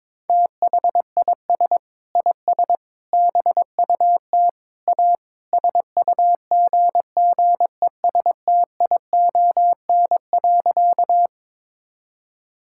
Morse code, heard as a keyed tone, about 22 wpm, 700 Hz. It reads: THIS IS BUT A SUGGESTION.
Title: Sentences from Top 1600 Words 22wpm